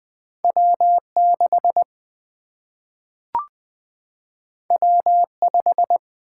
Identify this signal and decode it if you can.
Morse code, heard as a keyed tone, about 20 wpm, 700 Hz. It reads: W6 W5
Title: Single-Letter Single-Number 20wpm